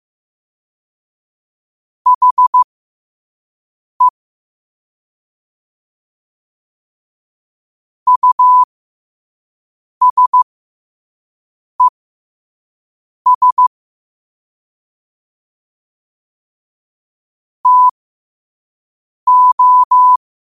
Morse code, heard as a keyed tone, about 15 wpm, spaced as for 5 wpm, 1000 Hz. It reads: HE USES TO
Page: New quotes every day in morse code at 5 Words per minute.